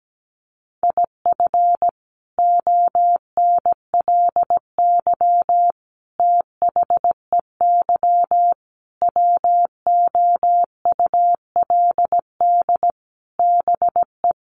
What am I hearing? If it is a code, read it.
Morse code, heard as a keyed tone, about 17 wpm, 700 Hz. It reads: IF ONLY THEY WOULD BE
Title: Sentences from Top 500 Words 17wpm